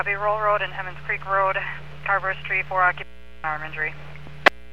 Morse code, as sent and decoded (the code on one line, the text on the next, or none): none